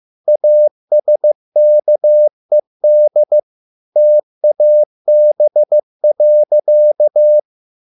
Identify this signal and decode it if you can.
Morse code, heard as a keyed tone, about 15 wpm, 600 Hz. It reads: ASKED TAB.